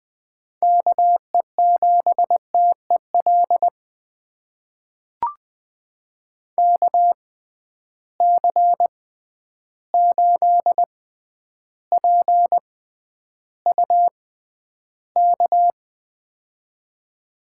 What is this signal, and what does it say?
Morse code, heard as a keyed tone, about 20 wpm, spaced as for 10 wpm, 700 Hz. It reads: KE7TEL KC8PUK
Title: USA Call Signs - 6x Character Spacing - 20wpm